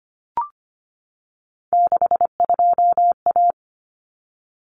Morse code, heard as a keyed tone, about 25 wpm, 700 Hz. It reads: 62A
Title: Common Cut Numbers - 3-Digit 25wpm